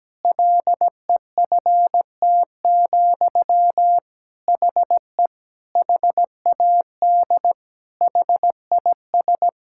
Morse code, heard as a keyed tone, about 17 wpm, 700 Hz. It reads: LEFT, HE HAD HIS
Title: Sentences from Top 1000 Words - Review Part 5 - 17wpm